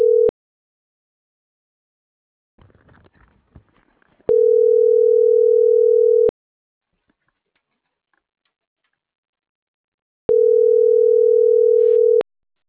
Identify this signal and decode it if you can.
Ringback tone (North America), 440 + 480 Hz.